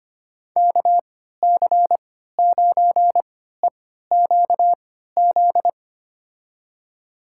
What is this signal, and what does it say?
Morse code, heard as a keyed tone, about 25 wpm, spaced as for 14 wpm, 700 Hz. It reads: KC9EQZ
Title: USA Call Signs - 6 Characters - 3x Character Spacing - 25wpm